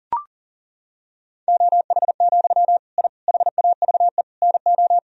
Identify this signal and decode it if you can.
Morse code, heard as a keyed tone, about 40 wpm, 700 Hz. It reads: OH, I HAVE NO
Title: Sentences from Top 1000 Words - Review Part 3 - 40wpm